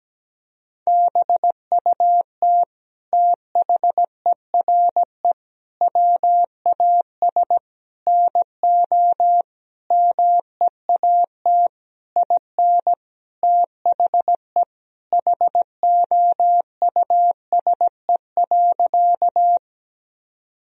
Morse code, heard as a keyed tone, about 17 wpm, 700 Hz. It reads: BUT THERE WAS NO MEAT IN THE HOUSE.
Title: Sentences from Top 1000 Words - Review Part 3 - 17wpm